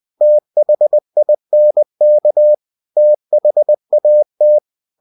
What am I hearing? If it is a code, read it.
Morse code, heard as a keyed tone, about 20 wpm, 600 Hz. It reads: THINK THAT